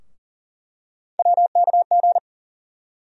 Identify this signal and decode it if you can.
Morse code, heard as a keyed tone, about 40 wpm, 700 Hz. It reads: WKG
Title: USA Call Sign Postfixes - 3 Letters 40wpm